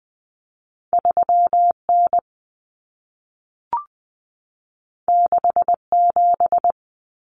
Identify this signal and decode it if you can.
Morse code, heard as a keyed tone, about 20 wpm, 700 Hz. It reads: 3N 67
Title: Common Cut Numbers - 2-Digit 20wpm